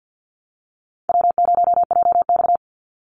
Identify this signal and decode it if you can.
Morse code, heard as a keyed tone, about 50 wpm, 700 Hz. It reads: W0JX
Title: Easy Call Signs Repeated 2x - 50wpm